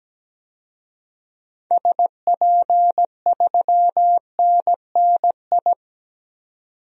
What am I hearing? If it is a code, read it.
Morse code, heard as a keyed tone, about 17 wpm, 700 Hz. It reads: SP3NNI
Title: DXCC-100 Call Signs - 6 Characters - 17wpm